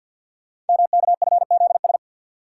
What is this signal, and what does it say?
Morse code, heard as a keyed tone, about 48 wpm, 700 Hz. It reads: MKFZS
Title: Code Groups - 5 Letters - 50wpm